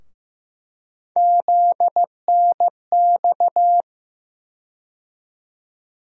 Morse code, heard as a keyed tone, about 15 wpm, 700 Hz. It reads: ZNX
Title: USA Call Sign Postfixes - 3 Letters 15wpm